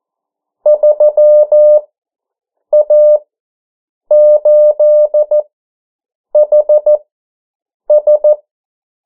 Morse code, heard as a keyed tone, about 14 wpm, spaced as for 7 wpm, 600 Hz. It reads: 3A8HS
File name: morse mp3 / 09 urok 8